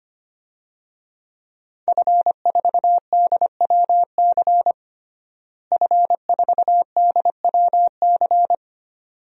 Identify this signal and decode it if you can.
Morse code, heard as a keyed tone, about 25 wpm, 700 Hz. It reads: F4DWC F4DWC
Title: DXCC-100 Call Signs - 5 Characters - Repeated 2x - 25wpm